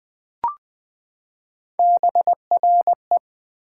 Morse code, heard as a keyed tone, about 20 wpm, 700 Hz. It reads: BRE
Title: Build A Word - 3 Letters At A Time - 6-Letter Words - 20wpm